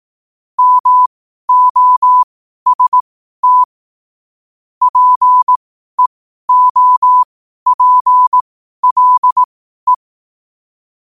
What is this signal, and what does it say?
Morse code, heard as a keyed tone, about 18 wpm, spaced as for 12 wpm, 1000 Hz. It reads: MOST PEOPLE
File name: qotdfiles / 12wpm